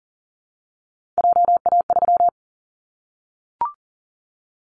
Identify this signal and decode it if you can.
Morse code, heard as a keyed tone, about 40 wpm, 700 Hz. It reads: JA3
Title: Callsign Prefixes for Top 50 DXCC - 40wpm